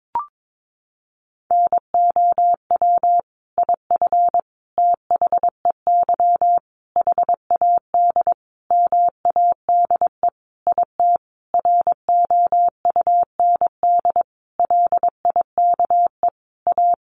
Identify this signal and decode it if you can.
Morse code, heard as a keyed tone, about 22 wpm, 700 Hz. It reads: NOW IF THEY HAD MADE IT ROUND LIKE A
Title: Sentences from Top 500 Words 22wpm